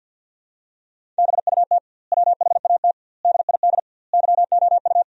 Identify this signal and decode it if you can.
Morse code, heard as a keyed tone, about 48 wpm, 700 Hz. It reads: BUT WHAT DID YOU
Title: 2x Word-Spacing - Sentences from Top 300 Words - 50wpm